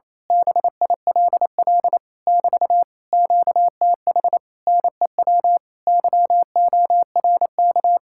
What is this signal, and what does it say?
Morse code, heard as a keyed tone, about 26 wpm, 700 Hz. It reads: BILL <BT> QTH NEW YORK
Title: Speed-Racing Common QSO Phrases Encore - 28wpm